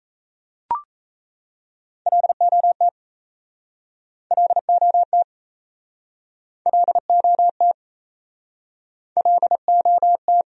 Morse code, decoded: LOT LOT LOT LOT

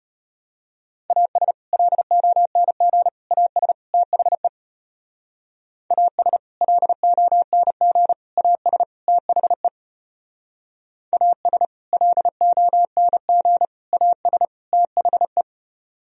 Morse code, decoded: AS LONG AS THE AS LONG AS THE AS LONG AS THE

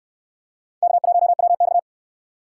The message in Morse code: K1WX

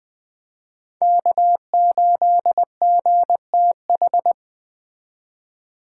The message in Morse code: K8GTH

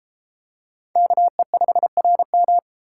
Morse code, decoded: KE5RM

